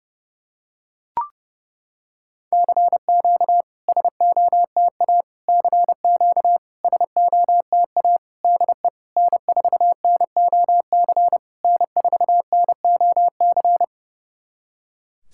Morse code, heard as a keyed tone, about 28 wpm, 690 Hz. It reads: CQ SOTA CQ SOTA DE N4NOC N4NOC
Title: SOTA QSO Practice 30wpm